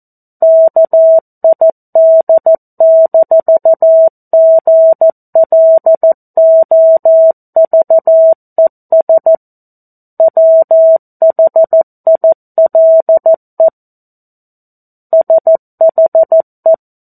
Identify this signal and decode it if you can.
Morse code, heard as a keyed tone, about 14 wpm, 650 Hz. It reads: KID-GLOVES WHILE SHE
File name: 02-Chapter-02 14 WPM